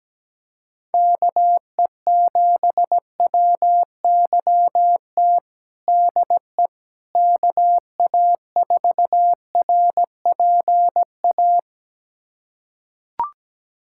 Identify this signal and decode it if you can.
Morse code, heard as a keyed tone, about 17 wpm, 700 Hz. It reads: KE7WYT DE KA4RPA
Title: USA Callsign DE Callsign 17wpm